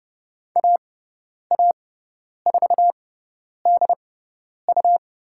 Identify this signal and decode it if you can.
Morse code, heard as a keyed tone, about 30 wpm, spaced as for 10 wpm, 700 Hz. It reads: AA4DU
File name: Lesson 18g - Letter m - Call Signs 30wpm